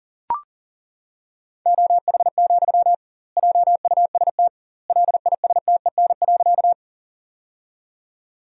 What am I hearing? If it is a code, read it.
Morse code, heard as a keyed tone, about 40 wpm, 700 Hz. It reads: OH, JUST LISTEN.
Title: 2x Word-Spacing - Sentences from Top 500 Words - 40wpm